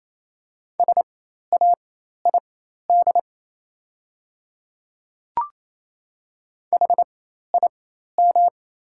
Morse code, heard as a keyed tone, about 28 wpm, spaced as for 13 wpm, 700 Hz. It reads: SAID HIM
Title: 4x Character Spacing - Top 100 Words - 28wpm